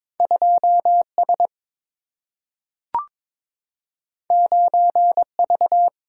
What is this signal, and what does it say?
Morse code, heard as a keyed tone, about 22 wpm, 700 Hz. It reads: 2 S 9 V
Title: Single-Letter Single-Number 22wpm